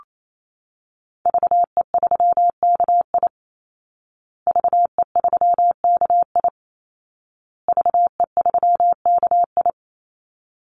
Morse code, heard as a keyed tone, about 28 wpm, 700 Hz. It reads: VE3KI VE3KI VE3KI
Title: Call Signs Repeated 2x - 28wpm